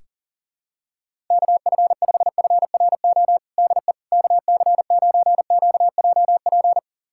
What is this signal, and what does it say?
Morse code, heard as a keyed tone, about 40 wpm, 700 Hz. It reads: KF5FRO DE KC9QJP
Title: USA Callsign DE Callsign - 6 Characters - 40wpm